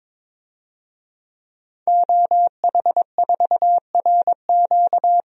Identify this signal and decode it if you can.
Morse code, heard as a keyed tone, about 22 wpm, 700 Hz. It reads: OH4RQ
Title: DXCC-100 Call Signs - 5 Characters - 22wpm